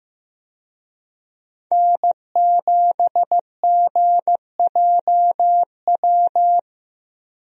Morse code, heard as a keyed tone, about 16 wpm, 700 Hz. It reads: N7GJW
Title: USA Call Signs - 5 Characters - Repeated 2x - 15wpm